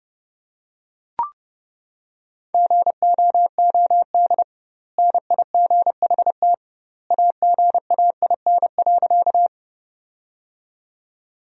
Morse code, heard as a keyed tone, about 32 wpm, 700 Hz. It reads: GOOD NIGHT AGAIN.
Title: 2x Word-Spacing - Sentences from Top 300 Words - 30wpm